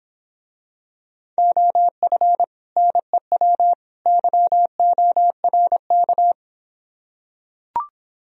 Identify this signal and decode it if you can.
Morse code, heard as a keyed tone, about 26 wpm, 700 Hz. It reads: OF NEW YORK